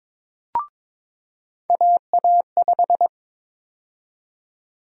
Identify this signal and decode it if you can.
Morse code, heard as a keyed tone, about 22 wpm, 700 Hz. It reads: AA5
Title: USA Call Sign Prefixes - 22wpm